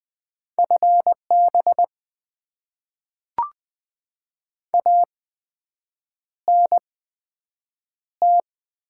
Morse code, heard as a keyed tone, about 20 wpm, 700 Hz. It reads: F B A N T
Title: Top 20 QSO Elements - 8x Character Spacing - 20wpm